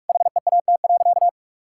Morse code, heard as a keyed tone, about 46 wpm, 700 Hz. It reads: HEAT.
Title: Sentences from Top 1000 Words - Review Part 6 - 45wpm